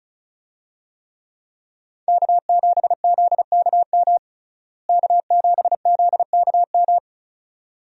Morse code, heard as a keyed tone, about 34 wpm, 700 Hz. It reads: K7ZKM K7ZKM
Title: USA Call Signs - 5 Characters - Repeated 2x - 35wpm